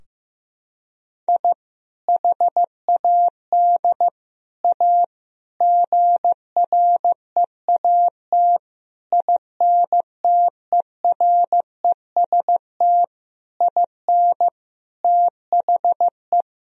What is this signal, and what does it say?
Morse code, heard as a keyed tone, about 15 wpm, 700 Hz. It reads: I HAD A GREAT INTEREST IN THE
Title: Sentences from Top 1000 Words - Review Part 2 - 15wpm